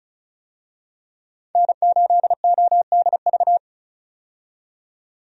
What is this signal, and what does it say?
Morse code, heard as a keyed tone, about 36 wpm, 700 Hz. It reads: N8ODV
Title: USA Call Signs - 5 Characters - 35wpm